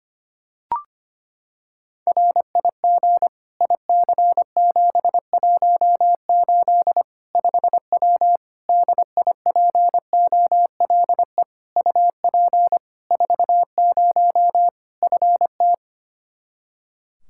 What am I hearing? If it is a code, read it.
Morse code, heard as a keyed tone, about 25 wpm, 700 Hz. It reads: RIG IC718 5W DIPOLE UP 40 FT
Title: K1USN SST QSO Practice 25wpm